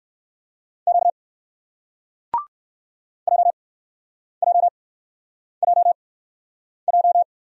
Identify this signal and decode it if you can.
Morse code, heard as a keyed tone, about 58 wpm, 700 Hz. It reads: Q J J J J